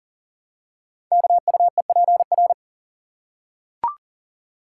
Morse code, heard as a keyed tone, about 38 wpm, 690 Hz. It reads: KUEPR
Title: Code Groups - 5 Letters - 40wpm